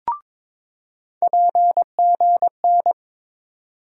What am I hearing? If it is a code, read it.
Morse code, heard as a keyed tone, about 22 wpm, 700 Hz. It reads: PGN
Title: USA Call Sign Postfixes - 3 Letters 22wpm